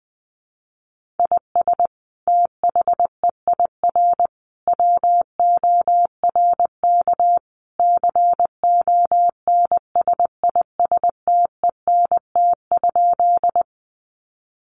Morse code, heard as a keyed tone, about 20 wpm, 700 Hz. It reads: IS THEIR WORK CONSISTENT?
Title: Sentences from Top 1500 Words 20wpm